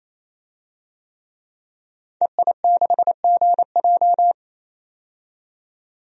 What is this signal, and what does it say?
Morse code, heard as a keyed tone, about 28 wpm, 700 Hz. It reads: EI6GJ
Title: DXCC-100 Call Signs - 5 Characters - 28wpm